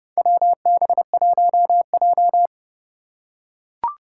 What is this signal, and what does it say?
Morse code, heard as a keyed tone, about 29 wpm, 700 Hz. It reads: WB1J
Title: USA Call Signs - 4 Characters - 2x Character Spacing - 30wpm